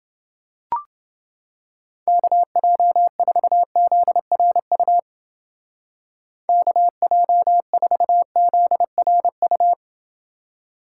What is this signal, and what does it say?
Morse code, heard as a keyed tone, about 28 wpm, 700 Hz. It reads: KJ4ZRU KJ4ZRU